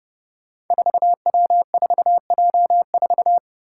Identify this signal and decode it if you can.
Morse code, heard as a keyed tone, about 28 wpm, 700 Hz. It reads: 4W4J4
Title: Five Challenging Letter-Number Combinations 30wpm